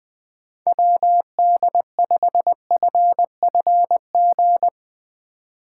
Morse code, heard as a keyed tone, about 20 wpm, 700 Hz. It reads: WD5FFG
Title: USA Call Signs - 6 Characters - 4x Character Spacing - 20wpm